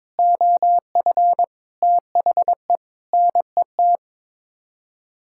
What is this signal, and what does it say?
Morse code, heard as a keyed tone, about 22 wpm, 700 Hz. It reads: OF THE NET
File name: Sets of 9 Words - 22wpm